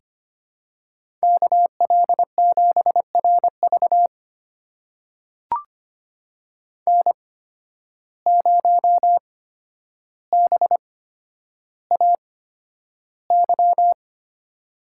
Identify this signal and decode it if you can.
Morse code, heard as a keyed tone, about 25 wpm, spaced as for 11 wpm, 700 Hz. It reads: KL7RV N0BAY
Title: USA Call Signs - 5 Characters - 8x Character Spacing - 25wpm